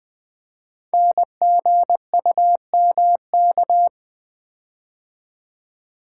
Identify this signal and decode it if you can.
Morse code, heard as a keyed tone, about 20 wpm, 700 Hz. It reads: NGUMK
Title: Code Groups - 5 Letters - 20wpm